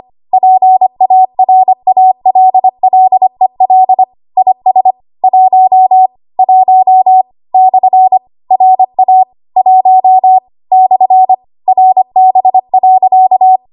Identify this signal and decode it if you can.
Morse code, heard as a keyed tone, about 25 wpm, 750 Hz. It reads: PARALLEL IS 1 1 / RA 1 / RB.
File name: Archive / 25 WPM / 25WPM